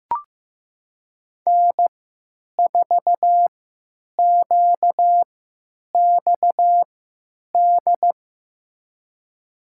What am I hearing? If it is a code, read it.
Morse code, heard as a keyed tone, about 15 wpm, spaced as for 9 wpm, 700 Hz. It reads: N4QXD
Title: USA Call Signs - 5 Characters - 3x Character Spacing - 15wpm